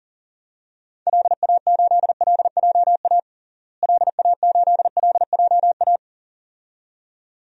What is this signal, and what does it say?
Morse code, heard as a keyed tone, about 37 wpm, 700 Hz. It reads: LA8LJA LA8LJA
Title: DXCC-100 Call Signs - 6 Characters - Repeated 2x - 40wpm